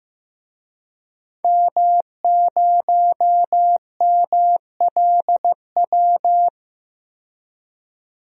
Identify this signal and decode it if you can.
Morse code, heard as a keyed tone, about 15 wpm, 700 Hz. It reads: M0MLW
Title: DXCC-100 Call Signs - 5 Characters - 15wpm